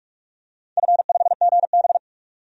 Morse code, heard as a keyed tone, about 46 wpm, 700 Hz. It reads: F5GB
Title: DXCC-100 Call Signs - 4 Characters - 45wpm